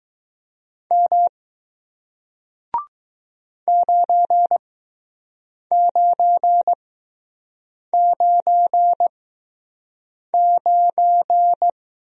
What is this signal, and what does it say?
Morse code, heard as a keyed tone, about 21 wpm, spaced as for 6 wpm, 700 Hz. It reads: M 9999